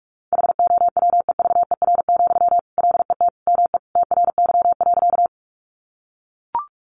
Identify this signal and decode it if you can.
Morse code, heard as a keyed tone, about 44 wpm, 700 Hz. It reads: HOWEVER, LET ME TRY.
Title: Sentences from Top 600 Words 45wpm